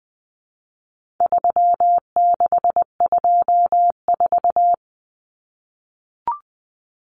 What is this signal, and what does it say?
Morse code, heard as a keyed tone, about 20 wpm, 700 Hz. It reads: 3624
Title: Common Cut Numbers - 4-Digit 20wpm